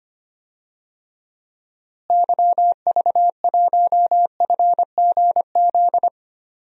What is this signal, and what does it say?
Morse code, heard as a keyed tone, about 25 wpm, 700 Hz. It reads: YV1FGZ
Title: DXCC-100 Call Signs - 6 Characters - Repeated 3x - 25wpm